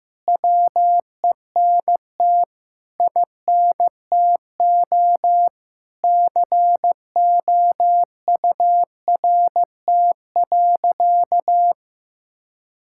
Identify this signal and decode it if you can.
Morse code, heard as a keyed tone, about 15 wpm, 700 Hz. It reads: WENT INTO COURT.
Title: Sentences from Top 1600 Words 15wpm